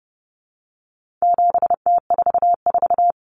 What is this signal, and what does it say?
Morse code, heard as a keyed tone, about 30 wpm, 700 Hz. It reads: 7T44
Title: Common Cut Numbers - 4-Digit 30wpm